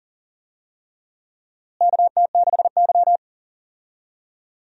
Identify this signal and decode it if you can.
Morse code, heard as a keyed tone, about 40 wpm, 700 Hz. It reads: KT6Y
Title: USA Call Signs - 4 Characters - Repeated 3x - 40wpm